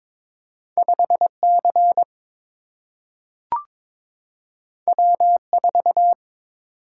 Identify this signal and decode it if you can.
Morse code, heard as a keyed tone, about 22 wpm, 700 Hz. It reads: 5C W4
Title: Single-Letter Single-Number 22wpm